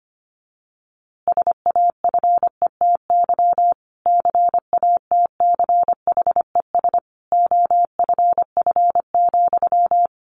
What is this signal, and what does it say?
Morse code, heard as a keyed tone, about 25 wpm, 700 Hz. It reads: SAFETY CATCHES OFF,
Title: A Sound Of Thunder 25wpm